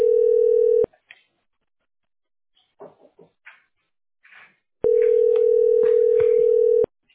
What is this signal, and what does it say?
Ringback tone (North America), 440 + 480 Hz.